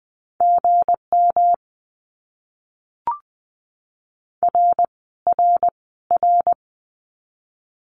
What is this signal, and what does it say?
Morse code, heard as a keyed tone, about 20 wpm, 700 Hz. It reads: GM R R R
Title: Abbreviations Repeated 3x - 20wpm